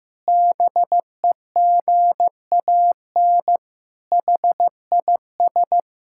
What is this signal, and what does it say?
Morse code, heard as a keyed tone, about 15 wpm, 700 Hz. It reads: BEGAN HIS
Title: Pangram Challenge v2 - 15wpm